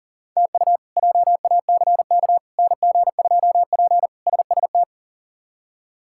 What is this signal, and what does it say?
Morse code, heard as a keyed tone, about 40 wpm, 700 Hz. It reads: TU JACK NG2P SST
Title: K1USN SST QSO Practice - 40wpm